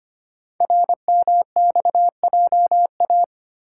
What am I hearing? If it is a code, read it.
Morse code, heard as a keyed tone, about 25 wpm, 700 Hz. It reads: RMXJA
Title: Code Groups - 5 Letters - 25wpm